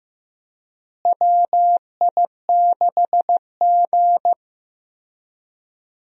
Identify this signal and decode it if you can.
Morse code, heard as a keyed tone, about 15 wpm, 700 Hz. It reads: WI6G
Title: USA Call Signs - 4 Characters - Repeated 3x - 15wpm